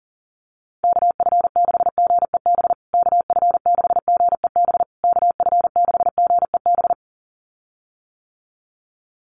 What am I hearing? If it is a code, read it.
Morse code, heard as a keyed tone, about 40 wpm, 700 Hz. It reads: KF6GEB KF6GEB KF6GEB
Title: USA Call Signs Repeated 3x 40wpm